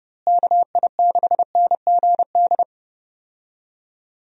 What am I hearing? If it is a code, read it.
Morse code, heard as a keyed tone, about 30 wpm, 700 Hz. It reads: KI6NGD
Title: USA Call Signs - 6 Characters - Repeated 3x - 30wpm